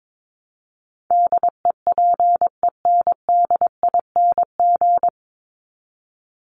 Morse code, heard as a keyed tone, about 22 wpm, 700 Hz. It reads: DEPENDING
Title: ING Suffix - Top 100 Words with Base Words 22wpm